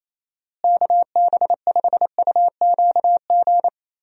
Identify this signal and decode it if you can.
Morse code, heard as a keyed tone, about 28 wpm, 700 Hz. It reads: KB5UQG
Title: USA Call Signs - 6 Characters - Repeated 3x - 28wpm